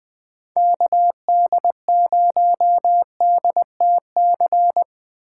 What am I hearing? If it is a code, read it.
Morse code, heard as a keyed tone, about 20 wpm, 700 Hz. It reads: KD0DTC
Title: USA Call Signs - 6 Characters - 2x Character Spacing - 20wpm